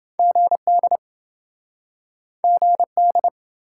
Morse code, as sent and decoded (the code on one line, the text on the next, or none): --. -.. / --. -..
GD GD